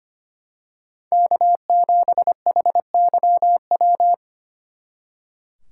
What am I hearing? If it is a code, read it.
Morse code, heard as a keyed tone, about 28 wpm, 700 Hz. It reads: K7HYW
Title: Speed Racing - USA Call Signs 25wpm